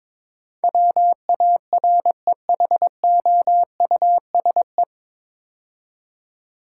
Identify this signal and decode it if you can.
Morse code, heard as a keyed tone, about 22 wpm, 700 Hz. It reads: WAREHOUSE
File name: 9 Letter Words - First 3 Letters Given - 22wpm